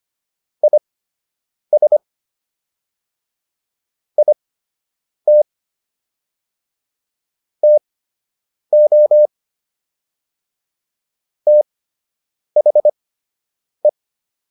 Morse code, heard as a keyed tone, about 25 wpm, spaced as for 8 wpm, 600 Hz. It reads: IS IT TO THE